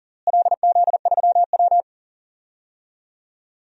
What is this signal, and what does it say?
Morse code, heard as a keyed tone, about 40 wpm, 700 Hz. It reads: LZ3W
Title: DXCC-100 Call Signs - 4 Characters - Repeated 2x - 40wpm